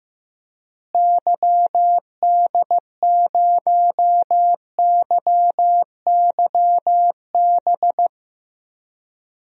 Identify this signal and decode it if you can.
Morse code, heard as a keyed tone, about 15 wpm, 700 Hz. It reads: YD0YYB
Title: DXCC-100 Call Signs - 6 Characters - Repeated 3x - 15wpm